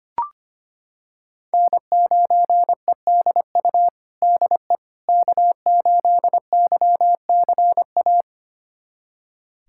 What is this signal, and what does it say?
Morse code, heard as a keyed tone, about 25 wpm, 700 Hz. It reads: N9EDU DE K8YCA
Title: USA Callsign DE Callsign - 5 Characters - 25wpm